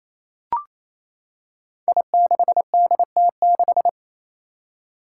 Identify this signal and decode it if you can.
Morse code, heard as a keyed tone, about 28 wpm, 700 Hz. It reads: I6DT6
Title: Five Challenging Letter-Number Combinations 28wpm